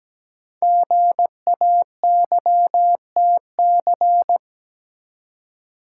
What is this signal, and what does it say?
Morse code, heard as a keyed tone, about 17 wpm, 700 Hz. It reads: GAYTC
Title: Code Groups - 5 Letters - 17wpm